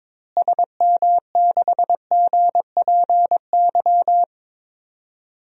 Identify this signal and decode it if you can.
Morse code, heard as a keyed tone, about 22 wpm, 700 Hz. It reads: SM6GPY
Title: DXCC-100 Call Signs - 6 Characters - Repeated 3x - 22wpm